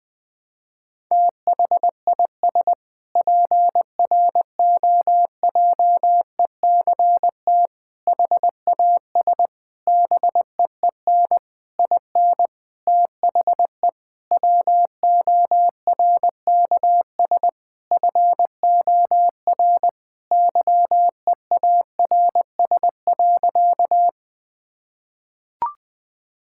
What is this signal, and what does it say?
Morse code, heard as a keyed tone, about 20 wpm, 700 Hz. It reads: THIS PROJECT HAS BEEN IN THE WORKS FOR YEARS.
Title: Sentences with 6-Word Phrases - 20wpm